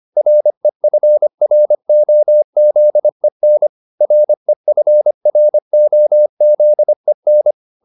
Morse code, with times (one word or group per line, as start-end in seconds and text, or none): 0.2-3.7 REFROZEN
4.0-7.5 REFROZEN